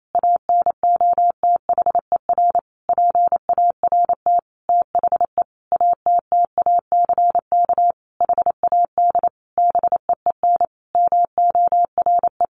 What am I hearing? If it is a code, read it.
Morse code, heard as a keyed tone, about 28 wpm, 700 Hz. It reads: ANOTHER PART THE ATTACK HAD BEEN MORE
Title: Sentences from Top 1500 Words 28wpm